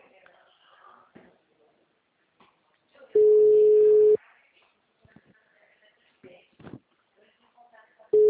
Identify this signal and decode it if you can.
Ringback tone (Europe, 425 Hz), with speech.